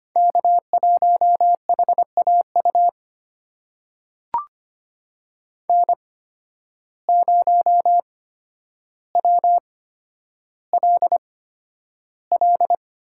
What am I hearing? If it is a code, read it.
Morse code, heard as a keyed tone, about 25 wpm, spaced as for 11 wpm, 700 Hz. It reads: K1HAU N0WLL
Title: USA Call Signs - 5 Characters - 8x Character Spacing - 25wpm